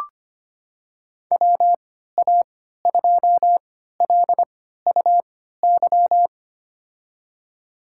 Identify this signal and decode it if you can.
Morse code, heard as a keyed tone, about 25 wpm, spaced as for 14 wpm, 700 Hz. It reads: WA2LUY
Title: USA Call Signs - 6 Characters - 3x Character Spacing - 25wpm